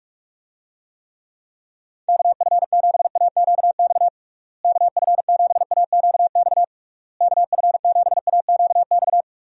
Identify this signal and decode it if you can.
Morse code, heard as a keyed tone, about 46 wpm, 700 Hz. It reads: KF7AQX KF7AQX KF7AQX
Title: USA Call Signs - 6 Characters - Repeated 3x - 45wpm